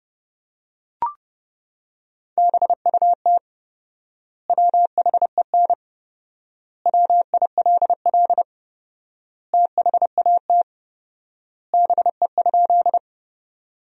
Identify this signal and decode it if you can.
Morse code, heard as a keyed tone, about 31 wpm, 700 Hz. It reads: BUT WHEN WILL THAT BE?
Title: 4x Word-Spacing - Sentences from Top 100 Words - 30wpm